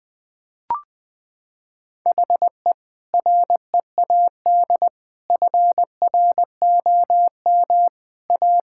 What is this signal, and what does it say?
Morse code, heard as a keyed tone, about 20 wpm, 700 Hz. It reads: HE READ FROM A
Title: Sentences with 2-Word Phrases - 20wpm